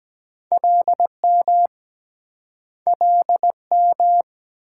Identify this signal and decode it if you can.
Morse code, heard as a keyed tone, about 24 wpm, 700 Hz. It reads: LM LM